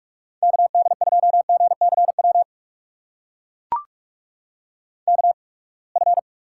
Morse code, heard as a keyed tone, about 44 wpm, spaced as for 31 wpm, 700 Hz. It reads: KD2GCW KF
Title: USA Call Signs - 6 Characters - 8x Character Spacing - 45wpm